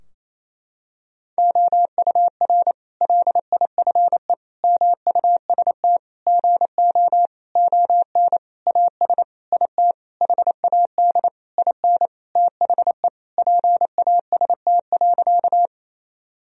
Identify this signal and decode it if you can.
Morse code, heard as a keyed tone, about 28 wpm, 700 Hz. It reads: OUR LIFE MUST GO ON AS IT HAD IN THE PAST.
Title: Sentences from Top 1000 Words - Review Part 3 - 28wpm